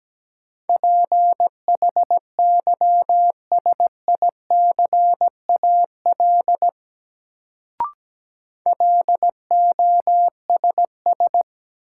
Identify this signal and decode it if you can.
Morse code, heard as a keyed tone, about 17 wpm, 700 Hz. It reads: PHYSICAL LOSS
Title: Sentences from Top 1400 Words - New Word Drill Alt - 17wpm